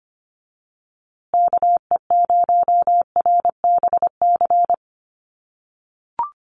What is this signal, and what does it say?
Morse code, heard as a keyed tone, about 25 wpm, 700 Hz. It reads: KE0RBC
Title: USA Call Signs Repeated 2x 25wpm